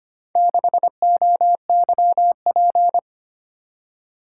Morse code, heard as a keyed tone, about 25 wpm, 700 Hz. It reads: 6OYP
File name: practice / 25wpm / Four Letter-Number Combinations - 25wpm